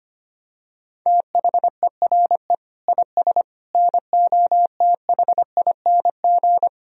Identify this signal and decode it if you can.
Morse code, heard as a keyed tone, about 25 wpm, 700 Hz. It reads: THERE IS NOTHING
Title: Sentences from Top 1000 Words - Review Part 1 - 25wpm